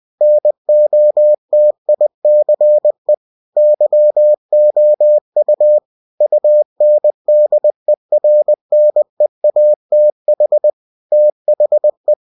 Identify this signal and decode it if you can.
Morse code, heard as a keyed tone, about 20 wpm, 600 Hz. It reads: NOTICE YOU UNDERNEATH THE
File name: cw-books / winnie-the-pooh / winnie-20wpm-page-0006